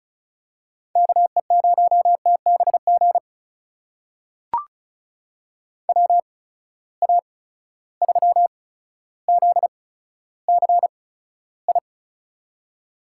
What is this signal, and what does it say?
Morse code, heard as a keyed tone, about 34 wpm, spaced as for 15 wpm, 700 Hz. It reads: KE0TBG WA3ZCI
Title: USA Call Signs - 6 Characters - 8x Character Spacing - 35wpm